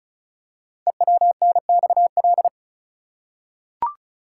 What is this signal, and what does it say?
Morse code, heard as a keyed tone, about 36 wpm, 690 Hz. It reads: EWNXL
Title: Code Groups - 5 Letters - 35wpm